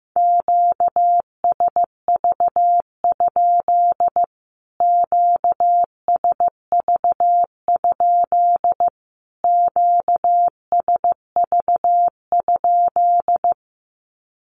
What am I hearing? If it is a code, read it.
Morse code, heard as a keyed tone, about 15 wpm, 700 Hz. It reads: QSV? QSV? QSV?
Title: Q-Codes Repeated 3x 15wpm